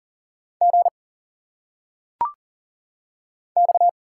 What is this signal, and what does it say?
Morse code, heard as a keyed tone, about 38 wpm, 700 Hz. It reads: G X
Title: Single Letters - Phonetic Pronunciation 40wpm